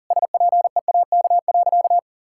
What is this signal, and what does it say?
Morse code, heard as a keyed tone, about 40 wpm, 700 Hz. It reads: SPEAK.
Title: Sentences from Top 1600 Words 40wpm